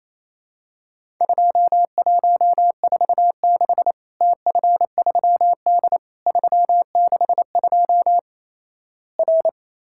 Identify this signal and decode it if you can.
Morse code, heard as a keyed tone, about 28 wpm, 700 Hz. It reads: 2146 TF3D 362 R
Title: Worked All Europe DX Contest - QTC Exchange - 28wpm